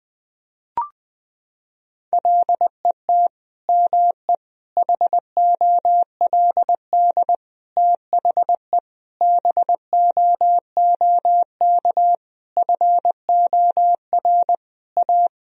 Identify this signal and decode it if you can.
Morse code, heard as a keyed tone, about 21 wpm, 700 Hz. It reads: LET ME HOLD THE BOOK FOR A
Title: Sentences from Top 1000 Words - Review Part 2 - 20wpm